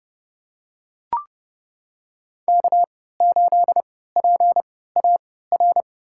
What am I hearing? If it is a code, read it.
Morse code, heard as a keyed tone, about 32 wpm, spaced as for 17 wpm, 700 Hz. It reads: K8PAR
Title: USA Call Signs - 5 Characters - 3x Character Spacing - 30wpm